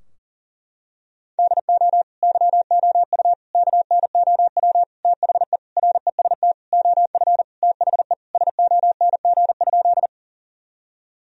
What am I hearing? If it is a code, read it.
Morse code, heard as a keyed tone, about 40 wpm, 700 Hz. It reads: DO YOU KNOW THE REST OF THE SONG?
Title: Sentences from Top 1000 Words - Review Part 2 - 40wpm